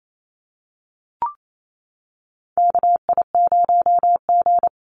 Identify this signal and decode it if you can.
Morse code, heard as a keyed tone, about 28 wpm, 700 Hz. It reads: KI0G
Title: NA SOTA Chaser Callsigns 28wpm